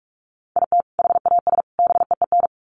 Morse code, heard as a keyed tone, about 44 wpm, 700 Hz. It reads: IT HAS BEEN
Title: Sets of 3 Words 45wpm